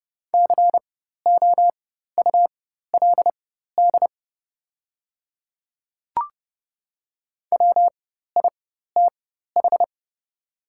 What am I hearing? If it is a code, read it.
Morse code, heard as a keyed tone, about 32 wpm, spaced as for 14 wpm, 700 Hz. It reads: COULD WITH
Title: 4x Character Spacing - Top 100 Words - 30wpm